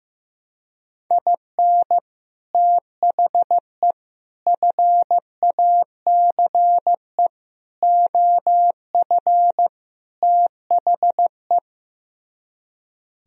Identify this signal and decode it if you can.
Morse code, heard as a keyed tone, about 15 wpm, 700 Hz. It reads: IN THE FACE OF THE
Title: Speed-Racing - Sets of 5 Words 15wpm